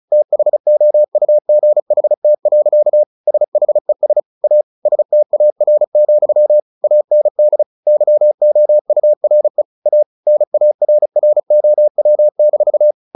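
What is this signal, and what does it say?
Morse code, heard as a keyed tone, about 34 wpm, 600 Hz. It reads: THOUGHT. SHES A STAR, AND YOURE A NARROW-
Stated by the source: Morse practice at 35 wpm from CW med Gnister, sending Ken Follett's novel Never in English.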